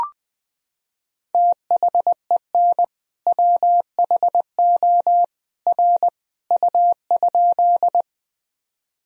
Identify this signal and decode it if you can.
Morse code, heard as a keyed tone, about 20 wpm, 700 Hz. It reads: THEN WHO R U?
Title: Sentences from Top 100 Words and Abbreviations 20wpm